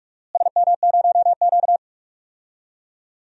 Morse code, heard as a keyed tone, about 42 wpm, 700 Hz. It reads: SM0Q
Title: DXCC-100 Call Signs - 4 Characters - Repeated 3x - 45wpm